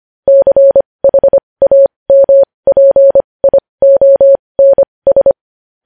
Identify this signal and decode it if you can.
Morse code, heard as a keyed tone, about 25 wpm, spaced as for 20 wpm, 550 Hz. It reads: CHAMPIONS
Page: news headlines in morse code at 20 wpm